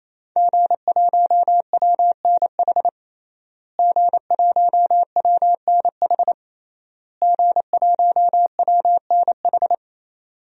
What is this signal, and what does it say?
Morse code, heard as a keyed tone, about 27 wpm, 700 Hz. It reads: G1WNH G1WNH G1WNH
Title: DXCC-100 Call Signs - 5 Characters - Repeated 3x - 28wpm